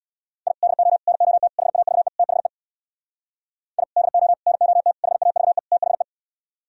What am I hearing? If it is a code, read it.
Morse code, heard as a keyed tone, about 64 wpm, 690 Hz. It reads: I DO NOT HAVE THE I DO NOT HAVE THE